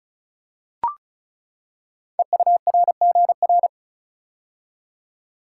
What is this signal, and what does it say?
Morse code, heard as a keyed tone, about 36 wpm, 700 Hz. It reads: EURGR
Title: Code Groups - 5 Letters - 35wpm